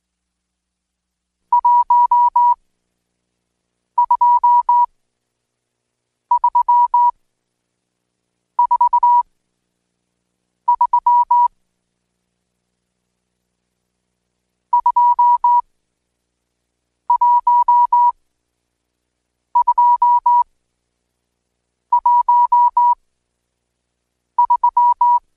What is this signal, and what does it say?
Morse code, heard as a keyed tone, about 20 wpm, spaced as for 5 wpm, 970 Hz. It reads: 12343 21213
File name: Code Quick / AudioDownld1 / Seg76-Day8-DayDrill 8